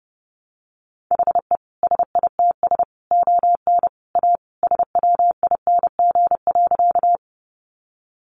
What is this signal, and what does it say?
Morse code, heard as a keyed tone, about 30 wpm, 700 Hz. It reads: HE SITS ON A SWING.
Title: Additional Easy Phrases 30wpm